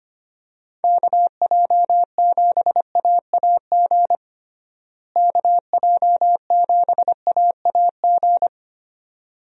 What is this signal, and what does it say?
Morse code, heard as a keyed tone, about 25 wpm, 700 Hz. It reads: KJ7AAG KJ7AAG
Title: USA Call Signs - 6 Characters - Repeated 2x - 25wpm